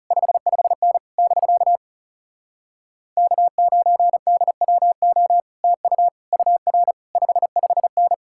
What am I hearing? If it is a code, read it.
Morse code, heard as a keyed tone, about 37 wpm, 700 Hz. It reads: 55N <BK> K9DWO TU UR 55N
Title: Speed-Racing Common QSO Phrases Encore - 30wpm